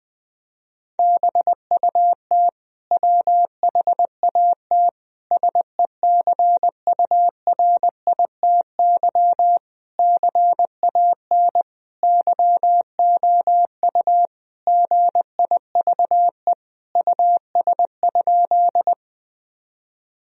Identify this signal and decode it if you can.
Morse code, heard as a keyed tone, about 20 wpm, 700 Hz. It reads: BUT WHAT SECURITY CAN YOU GIVE US?
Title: Sentences from Top 1000 Words - Review Part 4 - 20wpm